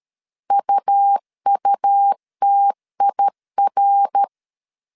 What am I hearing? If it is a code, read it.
Morse code, heard as a keyed tone, about 13 wpm, 800 Hz. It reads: UUTIR
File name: CW / DET / 1 3 1500C S B 12wpm 2828